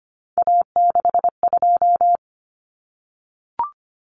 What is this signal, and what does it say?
Morse code, heard as a keyed tone, about 25 wpm, 700 Hz. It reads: A62
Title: Common Cut Numbers - 3-Digit 25wpm